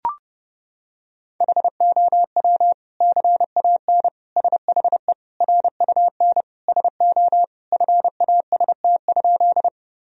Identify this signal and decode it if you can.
Morse code, heard as a keyed tone, about 30 wpm, 700 Hz. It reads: HOW CAN SHE RUN SO FAST?
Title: Sentences from Top 1000 Words - Review Part 2 - 30wpm